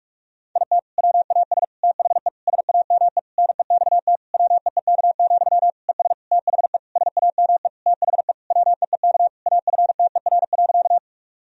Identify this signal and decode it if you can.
Morse code, heard as a keyed tone, about 41 wpm, 690 Hz. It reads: IT WAS THE SAME NEXT WEEK, ES THE SAME THE WEEK AFTER.
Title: Sentences from Top 500 Words and Abbreviations 45wpm